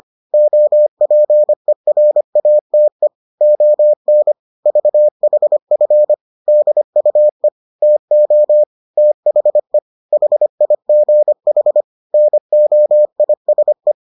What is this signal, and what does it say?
Morse code, heard as a keyed tone, about 25 wpm, 600 Hz. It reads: OPERATE ON VHF DUE TO THE HIGH NOISE